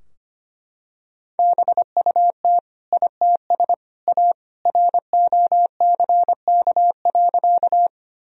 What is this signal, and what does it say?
Morse code, heard as a keyed tone, about 25 wpm, 700 Hz. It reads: BUT ITS A ROCK.
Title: Sentences from Top 1000 Words - Review Part 2 - 25wpm